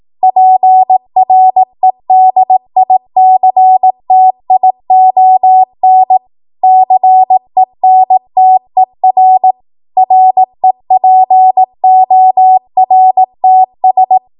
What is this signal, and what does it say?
Morse code, heard as a keyed tone, about 18 wpm, 750 Hz. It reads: PREDICTION CENTER REPORTS